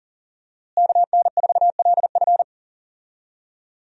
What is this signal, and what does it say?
Morse code, heard as a keyed tone, about 40 wpm, 700 Hz. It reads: KN4LF
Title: USA Call Signs - 5 Characters - Repeated 2x - 40wpm